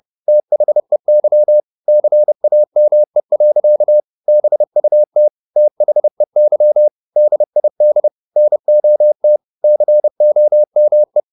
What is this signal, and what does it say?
Morse code, heard as a keyed tone, about 30 wpm, 600 Hz. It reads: THEY CAME. BUT THEY DID NOT COME